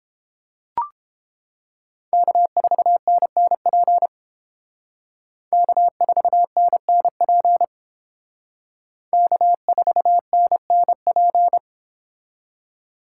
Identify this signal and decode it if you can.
Morse code, heard as a keyed tone, about 29 wpm, 700 Hz. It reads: K4NNP K4NNP K4NNP